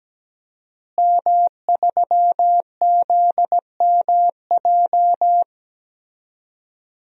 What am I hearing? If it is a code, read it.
Morse code, heard as a keyed tone, about 17 wpm, 700 Hz. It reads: M3ZMJ
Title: DXCC-100 Call Signs - 5 Characters - Repeated 3x - 17wpm